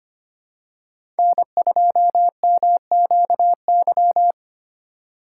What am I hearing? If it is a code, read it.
Morse code, heard as a keyed tone, about 25 wpm, 700 Hz. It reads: N2MQY
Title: USA Call Signs - 3x Character Spacing - 25wpm